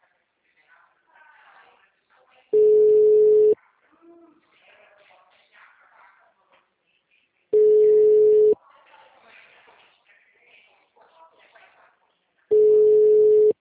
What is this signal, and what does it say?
Ringback tone (Europe, 425 Hz), with speech.